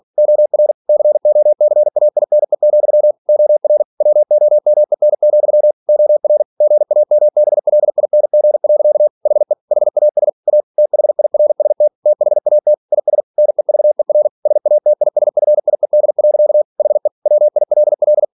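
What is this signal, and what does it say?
Morse code, heard as a keyed tone, about 46 wpm, 600 Hz. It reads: OR COCAINE, OR WOMEN, OR GAMBLING. HE HAS A THIRST THAT IS NEVER SATISFIED. HE WILL
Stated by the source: Practice text from an English novel sent at 47 WPM.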